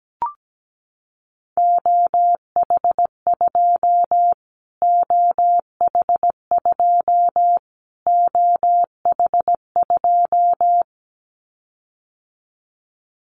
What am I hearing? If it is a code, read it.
Morse code, heard as a keyed tone, about 17 wpm, 700 Hz. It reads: OH2 OH2 OH2
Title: Callsign Prefixes for Top 25 DXCC - Repeated 3x - 17wpm